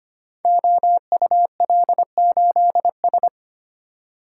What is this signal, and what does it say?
Morse code, heard as a keyed tone, about 25 wpm, 700 Hz. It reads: OUL8S
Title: Code Groups - 5 Letter:Numbers - 25wpm